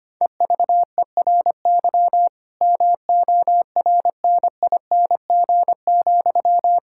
Morse code, heard as a keyed tone, about 25 wpm, 700 Hz. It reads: EVERY MORNING,
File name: practice / 25wpm / Difficult Sentences with Hints - 25wpm